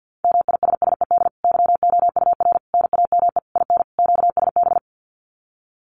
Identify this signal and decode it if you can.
Morse code, heard as a keyed tone, about 50 wpm, 700 Hz. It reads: MISSED YOUR NAME IN QSB
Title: Common QSO Phrases 50wpm